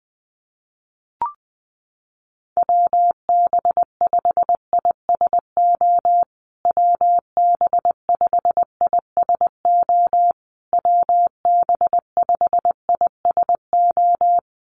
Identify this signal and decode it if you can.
Morse code, heard as a keyed tone, about 20 wpm, 700 Hz. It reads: WB5ISO WB5ISO WB5ISO
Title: USA Call Signs Repeated 3x 20wpm